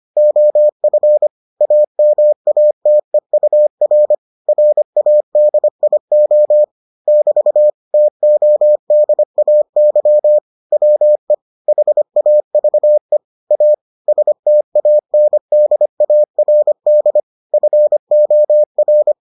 Morse code, heard as a keyed tone, about 25 wpm, 600 Hz. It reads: OF AMATEUR RADIO <BT> TODAY WE HAVE A STANDARD FOR